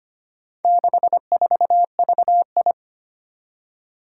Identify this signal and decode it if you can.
Morse code, heard as a keyed tone, about 25 wpm, 700 Hz. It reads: 64VI
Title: Four Challenging Letter-Number Combinations 25wpm